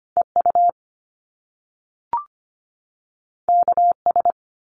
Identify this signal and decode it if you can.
Morse code, heard as a keyed tone, about 25 wpm, 700 Hz. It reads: E U K S
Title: Top Two Letter Combinations in Words 25wpm